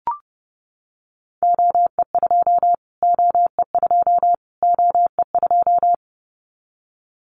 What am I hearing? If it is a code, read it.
Morse code, heard as a keyed tone, about 30 wpm, 700 Hz. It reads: OE2 OE2 OE2
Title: Callsign Prefixes for Top 25 DXCC - Repeated 3x - 30wpm